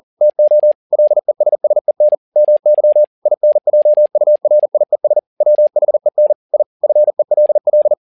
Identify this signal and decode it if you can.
Morse code, heard as a keyed tone, about 40 wpm, 600 Hz. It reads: TO LESSEN MY INJURIES WHEN I FELL
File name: learning center / tower trouble / tower touble 40wpm